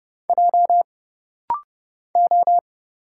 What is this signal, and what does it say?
Morse code, heard as a keyed tone, about 30 wpm, 700 Hz. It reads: J O
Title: Single Letter-Number Alt - Rapid-Fire 30wpm